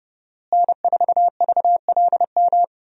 Morse code, heard as a keyed tone, about 30 wpm, 700 Hz. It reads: N4VLM
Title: USA Call Signs - 5 Characters - Repeated 2x - 30wpm